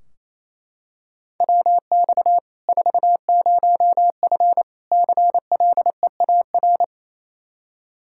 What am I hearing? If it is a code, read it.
Morse code, heard as a keyed tone, about 27 wpm, 700 Hz. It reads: WX 40F CLEAR
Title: Practice QSO - Exchange 2 - Info - 28wpm